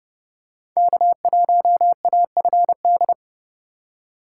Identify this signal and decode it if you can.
Morse code, heard as a keyed tone, about 31 wpm, 700 Hz. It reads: K1AFD
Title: USA Call Signs - 5 Characters - 2x Character Spacing - 30wpm